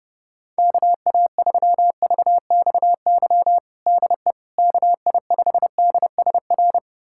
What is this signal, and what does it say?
Morse code, heard as a keyed tone, about 30 wpm, 700 Hz. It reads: KA3VXY DE KI5DSR
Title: USA Callsign DE Callsign 30wpm